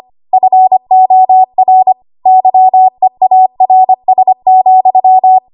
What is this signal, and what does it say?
Morse code, heard as a keyed tone, about 25 wpm, 750 Hz. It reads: FOR YEARS,
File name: Archive / 25 WPM / 25WPM